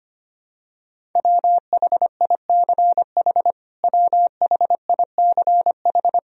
Morse code, heard as a keyed tone, about 25 wpm, 700 Hz. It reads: WHICH WHICH
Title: Top 100 Words Repeated 25wpm